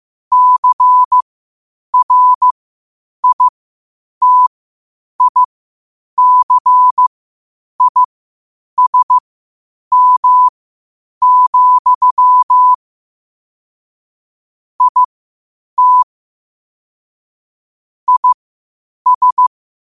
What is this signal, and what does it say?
Morse code, heard as a keyed tone, about 15 wpm, spaced as for 8 wpm, 1000 Hz. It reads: CRITICISM, IT IS
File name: qotdfiles / 8wpm